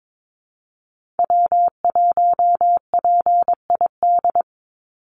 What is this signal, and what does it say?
Morse code, heard as a keyed tone, about 22 wpm, 700 Hz. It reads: W1PID
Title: Call Signs Repeated 3x - 22wpm